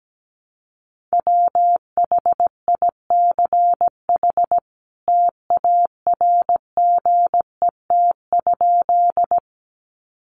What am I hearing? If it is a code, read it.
Morse code, heard as a keyed tone, about 17 wpm, 700 Hz. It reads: WHICH TARGET?
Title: Sentences from Top 1200 Words 17wpm